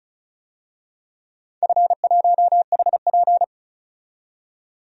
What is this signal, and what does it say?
Morse code, heard as a keyed tone, about 36 wpm, 700 Hz. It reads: F1HP
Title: DXCC-100 Call Signs - 4 Characters - 35wpm